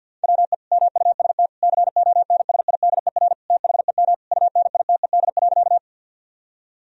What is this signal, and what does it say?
Morse code, heard as a keyed tone, about 50 wpm, 700 Hz. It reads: WE MUST CONSIDER THEM UNITED.
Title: Sentences from Top 1000 Words - Review Part 4 - 50wpm